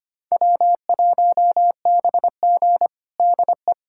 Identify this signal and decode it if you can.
Morse code, heard as a keyed tone, about 25 wpm, 700 Hz. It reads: W1BG DE
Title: USA Callsign DE Callsign - 4 Characters - 25wpm